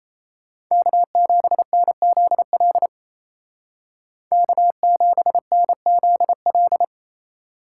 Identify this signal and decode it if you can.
Morse code, heard as a keyed tone, about 34 wpm, 700 Hz. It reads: K7NZL K7NZL
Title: Speed Racing - USA Call Signs 28wpm